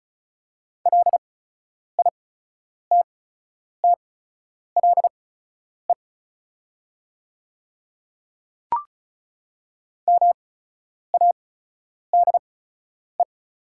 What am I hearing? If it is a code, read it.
Morse code, heard as a keyed tone, about 34 wpm, spaced as for 10 wpm, 700 Hz. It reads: LITTLE MADE
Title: 8x Character Spacing - Top 300 Words - 35wpm